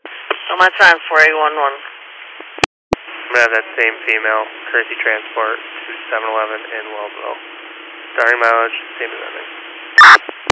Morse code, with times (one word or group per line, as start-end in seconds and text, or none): none